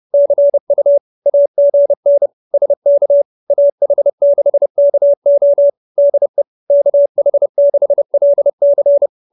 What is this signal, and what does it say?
Morse code, heard as a keyed tone, about 30 wpm, 570 Hz. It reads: CU AGN SK AH6KO DE KH6LC
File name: QSO311 30